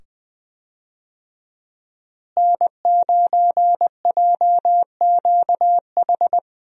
Morse code, heard as a keyed tone, about 21 wpm, 700 Hz. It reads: N9JQH
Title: USA Call Signs - 5 Characters - Repeated 2x - 20wpm